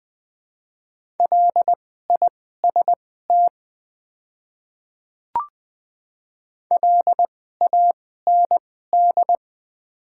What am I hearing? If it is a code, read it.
Morse code, heard as a keyed tone, about 21 wpm, spaced as for 15 wpm, 690 Hz. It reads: LIST LAND
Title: 2x Character Spacing - Top 500 Words - 20wpm